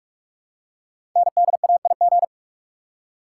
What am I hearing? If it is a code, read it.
Morse code, heard as a keyed tone, about 44 wpm, 700 Hz. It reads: NDAIG
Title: Code Groups - 5 Letters - 45wpm